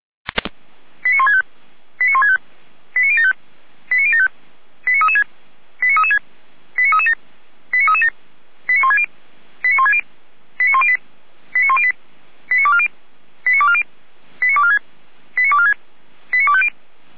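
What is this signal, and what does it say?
Five-tone selective call (ZVEI), heard as quick runs of five tones, tones of about 70 ms. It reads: 89165 89165 89075 89075 89336 89336 89337 89337 89160 89160 89108 89108 89240 89240 89246 89246 89250
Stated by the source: ZVEI five-tone selective call received by SDR, from audio.f85360.feuerwehr-orlamuende.org (fire brigade Orlamünde).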